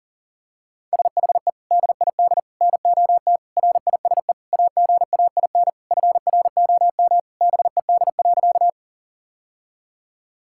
Morse code, heard as a keyed tone, about 40 wpm, 700 Hz. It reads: SHE DID NOT RISE AGAIN FROM BED.
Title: Sentences from Top 1000 Words - Review Part 6 - 40wpm